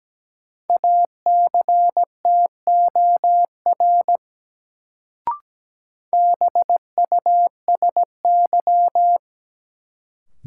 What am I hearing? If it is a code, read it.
Morse code, heard as a keyed tone, about 17 wpm, 700 Hz. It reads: ACTOR BUSY
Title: Sentences from Top 1500 Words - New Word Drill Alt - 17wpm